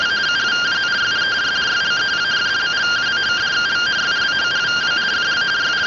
Two-tone teleprinter signal, about 170 Hz shift.